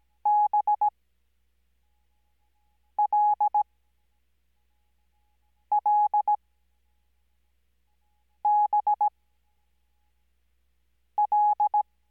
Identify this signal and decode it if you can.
Morse code, heard as a keyed tone, about 17 wpm, spaced as for 4 wpm, 840 Hz. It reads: BLLBL
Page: Farnsworth Spacing: 15WPM Character speed Spaced at 5WPM.